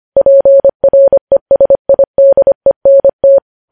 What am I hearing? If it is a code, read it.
Morse code, heard as a keyed tone, about 25 wpm, 550 Hz. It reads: PRESIDENT
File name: podcast / News headlines-1x-25x25